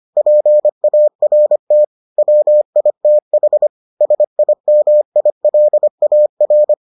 Morse code, heard as a keyed tone, about 25 wpm, 600 Hz. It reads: PART WITH SIMILAR